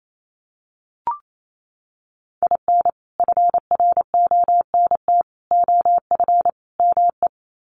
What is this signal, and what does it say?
Morse code, heard as a keyed tone, about 28 wpm, 700 Hz. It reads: IN FRONT OF ME
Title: Sets of 4 Words 28wpm